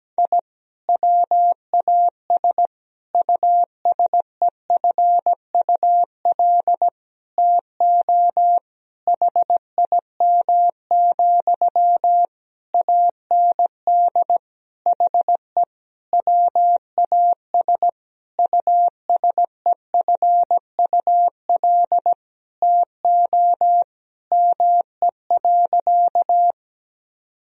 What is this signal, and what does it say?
Morse code, heard as a keyed tone, about 17 wpm, 700 Hz. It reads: I WAS USEFUL TO HIM, AND HE WAS USEFUL TO ME.
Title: Sentences from Top 1000 Words - Review Part 4 - 17wpm